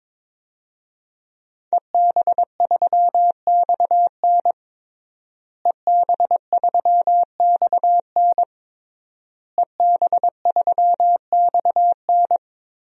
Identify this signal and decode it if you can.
Morse code, heard as a keyed tone, about 22 wpm, 700 Hz. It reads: EB3XN EB3XN EB3XN
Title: DXCC-100 Call Signs - 5 Characters - Repeated 3x - 22wpm